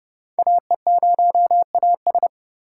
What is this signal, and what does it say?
Morse code, heard as a keyed tone, about 30 wpm, 700 Hz. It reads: AE0AS